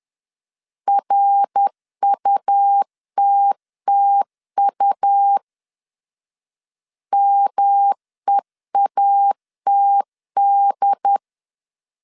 Morse code, heard as a keyed tone, about 10 wpm, 800 Hz. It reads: RUTTU MEATD
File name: CW / DET / 1 3 1500C S C 10wpm 3335